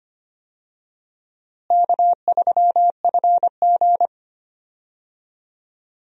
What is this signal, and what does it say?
Morse code, heard as a keyed tone, about 25 wpm, 700 Hz. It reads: K3FG
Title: USA Call Signs - 4 Characters - Repeated 2x - 25wpm